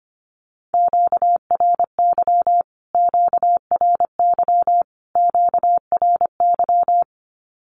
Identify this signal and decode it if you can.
Morse code, heard as a keyed tone, about 25 wpm, 700 Hz. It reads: QRY QRY QRY
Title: Q-Codes Repeated 3x 25wpm